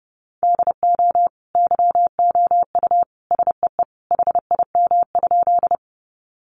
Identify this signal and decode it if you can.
Morse code, heard as a keyed tone, about 28 wpm, 700 Hz. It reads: DO YOU SEE HIM?
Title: Sentences from Top 100 Words 30wpm